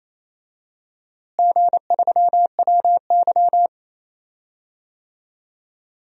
Morse code, heard as a keyed tone, about 28 wpm, 700 Hz. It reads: G3WY
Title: DXCC-100 Call Signs - 4 Characters - Repeated 3x - 28wpm